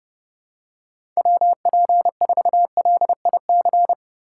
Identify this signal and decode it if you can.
Morse code, heard as a keyed tone, about 31 wpm, 700 Hz. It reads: WP4LIC